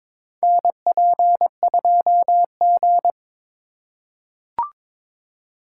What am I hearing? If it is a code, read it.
Morse code, heard as a keyed tone, about 22 wpm, 700 Hz. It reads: NP2G
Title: USA Call Signs - 4 Characters - 2x Character Spacing - 22wpm